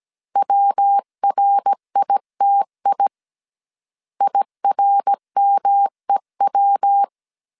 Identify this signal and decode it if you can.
Morse code, heard as a keyed tone, about 17 wpm, 800 Hz. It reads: WRITI IRMEW
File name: CW / DET / 1 3 1500C S C 16wpm 2148